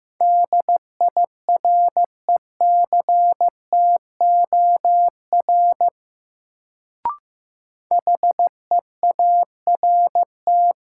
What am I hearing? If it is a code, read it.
Morse code, heard as a keyed tone, about 15 wpm, 700 Hz. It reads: DIRECTOR HEART
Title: Sets of 3 Words - New Word Drill - 15wpm